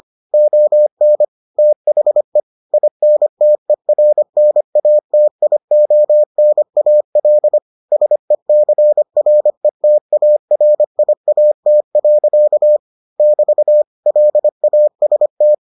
Morse code, heard as a keyed tone, about 25 wpm, 600 Hz. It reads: ON THE INTERNATIONAL SECRETARIAT. <BT> LAST